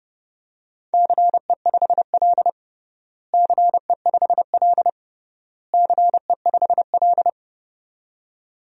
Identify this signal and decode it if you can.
Morse code, heard as a keyed tone, about 30 wpm, 700 Hz. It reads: CE5L CE5L CE5L
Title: DXCC-100 Call Signs - 4 Characters - Repeated 3x - 30wpm